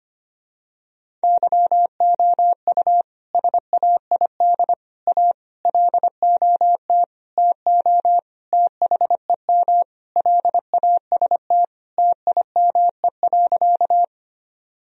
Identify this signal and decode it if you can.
Morse code, heard as a keyed tone, about 25 wpm, 700 Hz. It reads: YOU SAID A LOT TO THEM LAST TIME.
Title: Sentences from Top 1000 Words - Review Part 3 - 25wpm